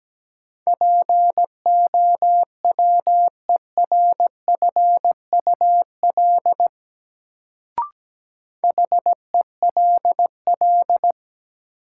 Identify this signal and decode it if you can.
Morse code, heard as a keyed tone, about 17 wpm, 700 Hz. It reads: POWERFUL HELL
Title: Sets of 5 Words Encore - New Word Drill - 17wpm